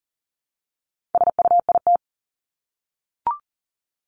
Morse code, heard as a keyed tone, about 40 wpm, 700 Hz. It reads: SUIT
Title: Top 1501-1750 Words 40wpm